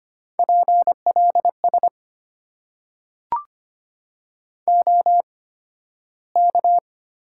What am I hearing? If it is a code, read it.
Morse code, heard as a keyed tone, about 25 wpm, spaced as for 13 wpm, 700 Hz. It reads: PLS OK
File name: Top 90 QSO Elements - 8x Character Spacing - 25wpm